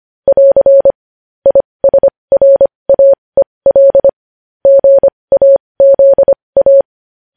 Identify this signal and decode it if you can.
Morse code, heard as a keyed tone, about 25 wpm, spaced as for 20 wpm, 550 Hz. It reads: <AR> ISRAEL GAZA
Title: News Headlines at 20